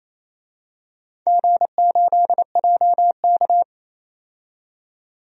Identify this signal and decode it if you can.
Morse code, heard as a keyed tone, about 28 wpm, 700 Hz. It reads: G8JK
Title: DXCC-100 Call Signs - 4 Characters - 28wpm